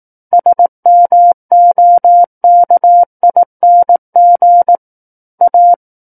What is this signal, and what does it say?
Morse code, heard as a keyed tone, about 18 wpm, 700 Hz. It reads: SMOKING A
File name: 18 WPM